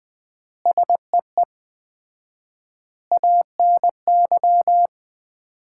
Morse code, heard as a keyed tone, about 20 wpm, 700 Hz. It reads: SEE ANY
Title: 4x Word-Spacing - Sets of 5 Words - 20wpm